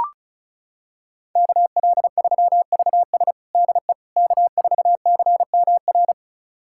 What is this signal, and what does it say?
Morse code, heard as a keyed tone, about 36 wpm, 700 Hz. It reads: KL3VS DE K4CMR
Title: USA Callsign DE Callsign - 5 Characters - 35wpm